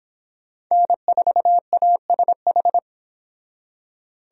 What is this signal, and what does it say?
Morse code, heard as a keyed tone, about 26 wpm, 700 Hz. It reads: N4ASH
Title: Speed Racing - USA 1x3 Call Signs 22wpm